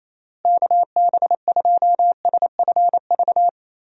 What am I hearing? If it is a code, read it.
Morse code, heard as a keyed tone, about 27 wpm, 700 Hz. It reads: KB2SFV
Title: USA Call Signs - 6 Characters - Repeated 2x - 28wpm